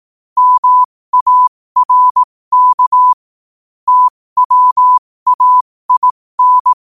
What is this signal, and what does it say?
Morse code, heard as a keyed tone, about 18 wpm, spaced as for 15 wpm, 1000 Hz. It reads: MARK TWAIN